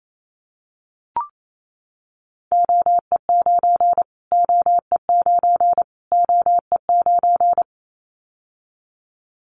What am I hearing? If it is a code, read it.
Morse code, heard as a keyed tone, about 28 wpm, 700 Hz. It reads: OE9 OE9 OE9
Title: Callsign Prefixes for Top 25 DXCC - Repeated 3x - 28wpm